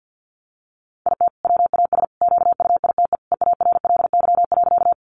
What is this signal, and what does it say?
Morse code, heard as a keyed tone, about 50 wpm, 700 Hz. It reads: IT WAS QUITE EARLY.
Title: Sentences from Top 700 Words 50wpm